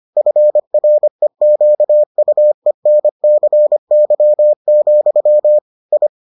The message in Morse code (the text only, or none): FREQUENCY, I